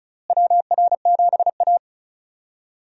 Morse code, heard as a keyed tone, about 36 wpm, 700 Hz. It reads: WR7A
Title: USA Call Signs - 4 Characters - Repeated 2x - 35wpm